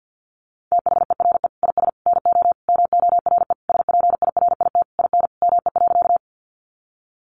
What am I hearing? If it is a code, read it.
Morse code, heard as a keyed tone, about 50 wpm, 700 Hz. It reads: THERE IS NO MORE SPIRIT IN ME.
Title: Sentences from Top 1400 Words 50wpm